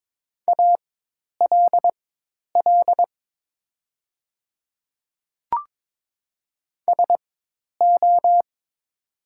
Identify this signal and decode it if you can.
Morse code, heard as a keyed tone, about 22 wpm, spaced as for 10 wpm, 700 Hz. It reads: ALL SO
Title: 4x Character Spacing - Top 100 Words - 22wpm